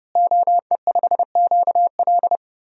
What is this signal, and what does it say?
Morse code, heard as a keyed tone, about 32 wpm, 700 Hz. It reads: OE5QL
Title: DXCC-100 Call Signs - 5 Characters - Repeated 2x - 30wpm